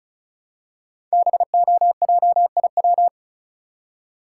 Morse code, heard as a keyed tone, about 36 wpm, 700 Hz. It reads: BOJIW
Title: Code Groups - 5 Letters - 35wpm